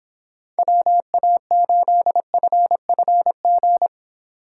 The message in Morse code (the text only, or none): WA8FFG